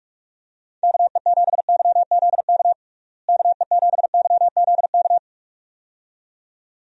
Morse code, heard as a keyed tone, about 46 wpm, 700 Hz. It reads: KE7YZK KE7YZK
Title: USA Call Signs - 6 Characters - Repeated 2x - 45wpm